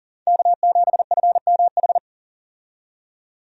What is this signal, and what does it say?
Morse code, heard as a keyed tone, about 40 wpm, 700 Hz. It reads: K7FMH
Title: USA Call Signs - 5 Characters - 40wpm